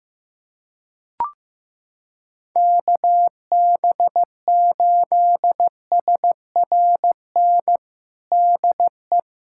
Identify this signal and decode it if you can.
Morse code, heard as a keyed tone, about 15 wpm, 700 Hz. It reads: KB8SRN DE
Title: USA Callsign DE Callsign - 6 Characters - 15wpm